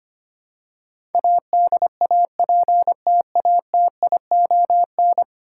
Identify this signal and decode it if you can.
Morse code, heard as a keyed tone, about 25 wpm, 700 Hz. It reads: ADAPTATION
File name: 10 Letter Words - First 3 Letters Given - 25wpm